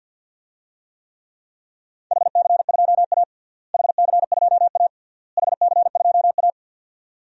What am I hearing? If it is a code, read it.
Morse code, heard as a keyed tone, about 52 wpm, 700 Hz. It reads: HC2A HC2A HC2A
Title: DXCC-100 Call Signs - 4 Characters - Repeated 3x - 50wpm